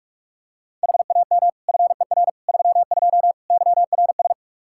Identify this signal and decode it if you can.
Morse code, heard as a keyed tone, about 44 wpm, 700 Hz. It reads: HAM FER 32 YRS
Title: Practice QSO - Exchange 3 - More Info - 45wpm